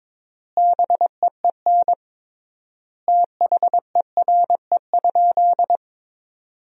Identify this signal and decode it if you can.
Morse code, heard as a keyed tone, about 21 wpm, 700 Hz. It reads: BEEN THERE?
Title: 3x Word-Spacing - Sentences from Top 100 Words - 22wpm